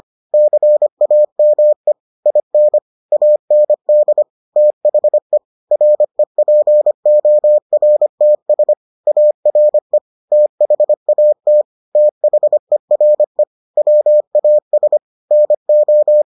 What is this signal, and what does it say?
Morse code, heard as a keyed tone, about 25 wpm, 600 Hz. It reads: CAME IN AND THE REPORTS ARE THAT THERE WAS NO